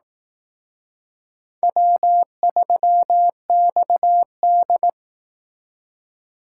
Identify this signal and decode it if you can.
Morse code, heard as a keyed tone, about 18 wpm, 700 Hz. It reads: W3XD
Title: Speed Racing - USA Call Signs 15wpm